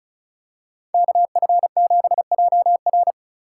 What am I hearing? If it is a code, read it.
Morse code, heard as a keyed tone, about 36 wpm, 700 Hz. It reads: KF7JR
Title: USA Call Signs - 2x Character Spacing - 35wpm